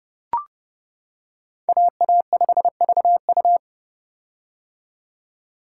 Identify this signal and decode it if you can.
Morse code, heard as a keyed tone, about 30 wpm, 700 Hz. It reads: AA5VU
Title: K1USN SST QSO Practice - 30wpm